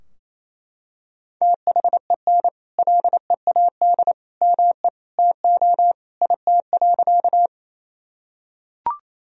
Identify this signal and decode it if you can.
Morse code, heard as a keyed tone, about 28 wpm, 700 Hz. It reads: THEN LEAD ME TO IT.
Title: Sentences from Top 1000 Words - Review Part 2 - 28wpm